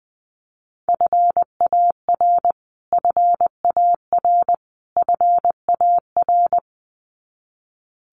Morse code, heard as a keyed tone, about 20 wpm, 700 Hz. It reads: FAR FAR FAR
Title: Top 500 Words Repeated 20wpm